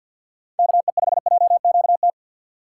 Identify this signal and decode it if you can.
Morse code, heard as a keyed tone, about 51 wpm, 700 Hz. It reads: KE5JQT
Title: USA Call Signs - 4x Character Spacing - 50wpm